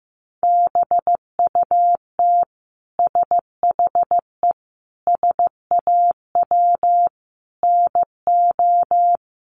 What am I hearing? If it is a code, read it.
Morse code, heard as a keyed tone, about 15 wpm, 700 Hz. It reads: BUT SHE SAW NO
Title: Sentences from Top 300 Words 15wpm